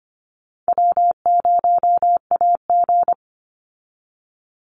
Morse code, heard as a keyed tone, about 25 wpm, 700 Hz. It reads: W0AG
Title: Easy Call Signs - 25wpm